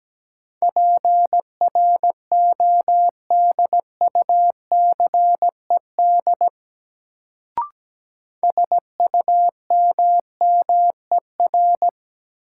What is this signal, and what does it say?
Morse code, heard as a keyed tone, about 17 wpm, 700 Hz. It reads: PRODUCED SUMMER
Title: Sentences from Top 1000 Words - Review Part 6 - New Word Drill - 17wpm